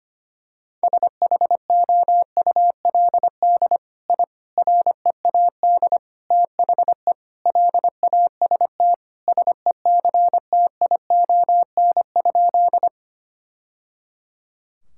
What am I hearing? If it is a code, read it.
Morse code, heard as a keyed tone, about 25 wpm, 700 Hz. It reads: SHOULD I READ THE LAST SECTION?
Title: Sentences from Top 1000 Words - Review Part 5 - 25wpm